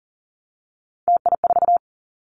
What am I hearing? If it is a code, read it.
Morse code, heard as a keyed tone, about 39 wpm, 700 Hz. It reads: TI4
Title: Callsign Prefixes for Top 100 DXCC - 40wpm